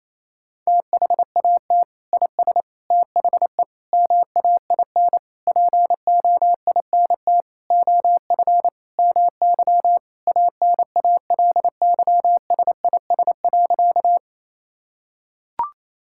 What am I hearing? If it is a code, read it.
Morse code, heard as a keyed tone, about 28 wpm, 700 Hz. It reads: THAT IS THE MAIN POINT OF MY ANALYSIS.
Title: Sentences from Top 1000 Words - Review Part 5 - 28wpm